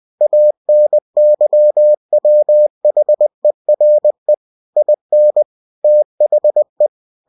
Morse code, heard as a keyed tone, about 20 wpm, 600 Hz. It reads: ANYWHERE IN THE
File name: cw-books / winnie-the-pooh / winnie-20wpm-page-0115